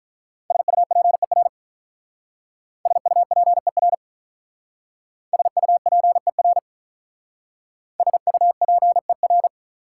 Morse code, heard as a keyed tone, about 46 wpm, 700 Hz. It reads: SUPER SUPER SUPER SUPER